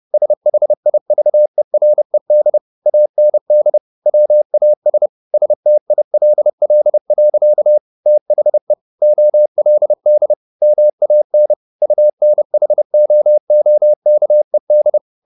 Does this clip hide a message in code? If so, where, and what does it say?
Yes: Morse code from 0.1 to 15.0 s, heard as a keyed tone, about 30 wpm, 600 Hz. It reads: SHIVERED AND WAS STILL. THE OLD MAN UNHOOKED